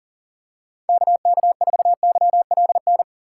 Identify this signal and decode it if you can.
Morse code, heard as a keyed tone, about 37 wpm, 700 Hz. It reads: KK4YLN
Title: USA Call Signs - 6 Characters - 4x Character Spacing - 40wpm